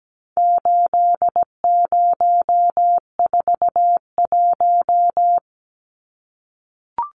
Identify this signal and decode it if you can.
Morse code, heard as a keyed tone, about 17 wpm, 700 Hz. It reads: 8041
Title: Numbers - 4-Digit 17wpm